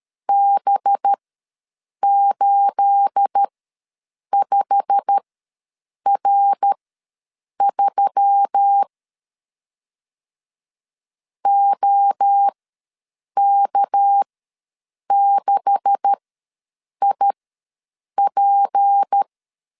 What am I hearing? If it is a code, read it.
Morse code, heard as a keyed tone, about 13 wpm, spaced as for 7 wpm, 800 Hz. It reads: B85R3 OK6IP
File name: LEZIONI / 9 286 ws 12wpm 3351